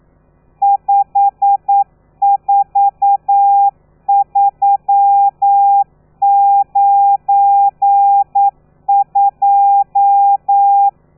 Morse code, heard as a keyed tone, about 9 wpm, 800 Hz. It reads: 54392
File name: telegrafia / i2tfj / Lez12 40